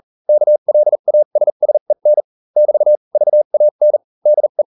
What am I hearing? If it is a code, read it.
Morse code, heard as a keyed tone, about 40 wpm, spaced as for 35 wpm, 600 Hz. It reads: KLASSEN <BT> VAN DE